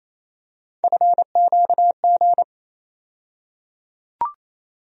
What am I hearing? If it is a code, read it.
Morse code, heard as a keyed tone, about 28 wpm, 700 Hz. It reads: FQG
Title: USA Call Sign Postfixes - 3 Letters 28wpm